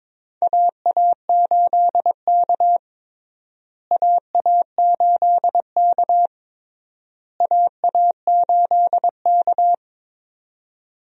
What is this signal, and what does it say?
Morse code, heard as a keyed tone, about 22 wpm, 700 Hz. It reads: AA8K AA8K AA8K
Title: USA Call Signs - 4 Characters - Repeated 3x - 22wpm